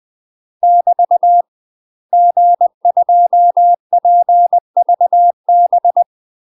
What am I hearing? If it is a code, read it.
Morse code, heard as a keyed tone, about 20 wpm, 700 Hz. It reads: <BT> G2PVB
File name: UK MASTER QSOs TRAINING / 20WPM / UK MASTER QSOs 0038